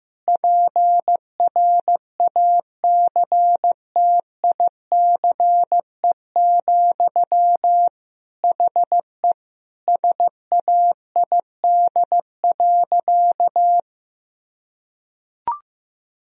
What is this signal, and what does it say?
Morse code, heard as a keyed tone, about 15 wpm, 700 Hz. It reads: PRACTICE, HE SAID.
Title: Sentences from Top 1000 Words - Review Part 4 - 15wpm